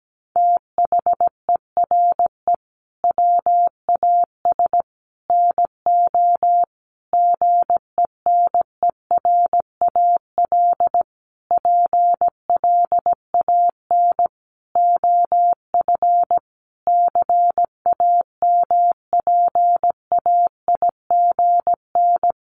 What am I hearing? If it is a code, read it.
Morse code, heard as a keyed tone, about 17 wpm, 700 Hz. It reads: THERE WAS NO GENERAL PLAN OF CAMPAIGN
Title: Sentences from Top 1200 Words 17wpm